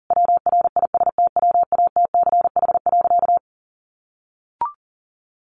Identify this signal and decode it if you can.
Morse code, heard as a keyed tone, about 40 wpm, 700 Hz. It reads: WRISTWATCH.
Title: A Sound Of Thunder 40wpm